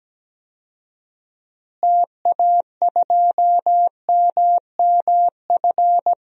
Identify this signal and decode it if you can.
Morse code, heard as a keyed tone, about 17 wpm, 700 Hz. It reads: TA2MMF
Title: DXCC-100 Call Signs - 6 Characters - Repeated 3x - 17wpm